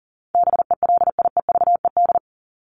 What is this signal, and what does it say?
Morse code, heard as a keyed tone, about 40 wpm, 700 Hz. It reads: BELIEVED
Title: ED Suffix - Top 100 Words with Base Words 40wpm